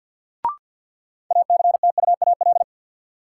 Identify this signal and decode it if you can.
Morse code, heard as a keyed tone, about 51 wpm, 700 Hz. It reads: ACTUAL
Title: Sentences from Top 1000 Words - Review Part 5 - New Word Drill - 50wpm